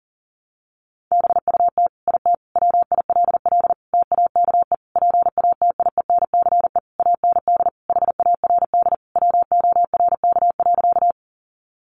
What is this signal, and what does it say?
Morse code, heard as a keyed tone, about 40 wpm, 700 Hz. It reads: BUT IT WILL TAKE PATIENCE AND HARD WORK.
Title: Sentences from Top 1300 Words 40wpm